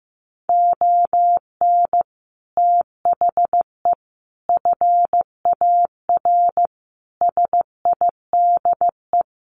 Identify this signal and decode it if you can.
Morse code, heard as a keyed tone, about 15 wpm, 700 Hz. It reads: ON THE FAR SIDE
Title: Sets of 5 Words 15wpm